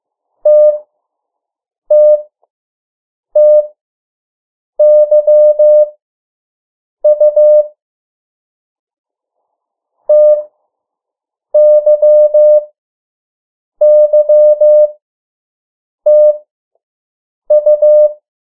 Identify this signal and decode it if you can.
Morse code, heard as a keyed tone, about 15 wpm, spaced as for 6 wpm, 600 Hz. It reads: TTTYU TYYTU
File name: morse mp3 / 03 urok 2